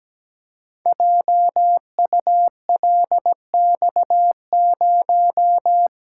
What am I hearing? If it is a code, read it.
Morse code, heard as a keyed tone, about 17 wpm, 700 Hz. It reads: JULX0
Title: Code Groups - 5 Letter:Numbers - 17wpm